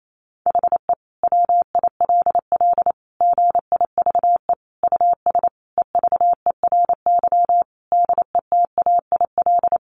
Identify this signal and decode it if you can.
Morse code, heard as a keyed tone, about 28 wpm, 700 Hz. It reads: HE WILL GIVE US EVERY DETAIL
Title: Sentences from Top 1500 Words 28wpm